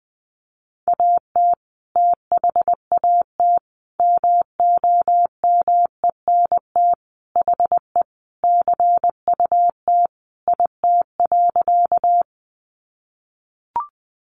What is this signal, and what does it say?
Morse code, heard as a keyed tone, about 20 wpm, 700 Hz. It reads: AT THAT MOMENT HE CUT IT.
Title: Sentences from Top 1000 Words 20wpm